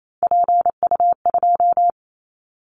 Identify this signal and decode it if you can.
Morse code, heard as a keyed tone, about 28 wpm, 700 Hz. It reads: PU2
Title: Callsign Prefixes for Top 50 DXCC - 28wpm